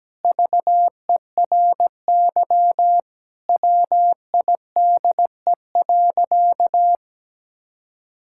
Morse code, heard as a keyed tone, about 17 wpm, 700 Hz. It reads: VERY WIDE.
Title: Sentences from Top 1000 Words - Review Part 6 - 17wpm